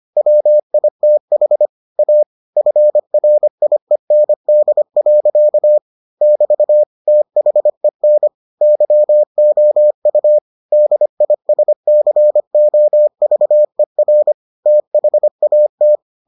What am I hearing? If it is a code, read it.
Morse code, heard as a keyed tone, about 25 wpm, 600 Hz. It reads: WITH A FRIEND. <BT> THEN YOU DISCOVER THAT